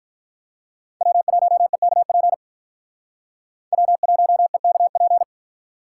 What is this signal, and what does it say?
Morse code, heard as a keyed tone, about 42 wpm, 700 Hz. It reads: W1EKP W1EKP
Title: Speed Racing - USA 1x3 Call Signs 35wpm